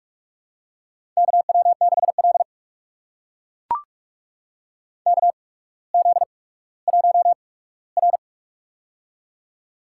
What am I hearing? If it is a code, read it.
Morse code, heard as a keyed tone, about 44 wpm, spaced as for 19 wpm, 700 Hz. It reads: KW6L KZ1R
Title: USA Call Signs - 4 Characters - 8x Character Spacing - 45wpm